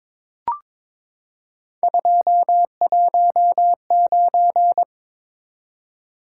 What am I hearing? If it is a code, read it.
Morse code, heard as a keyed tone, about 23 wpm, 700 Hz. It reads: 219
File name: practice / 22wpm / Three Letter-Number Combinations - 22wpm